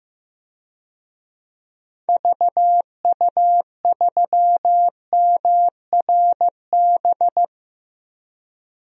Morse code, heard as a keyed tone, about 15 wpm, 700 Hz. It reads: VU3MRB
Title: DXCC-100 Call Signs - 6 Characters - 15wpm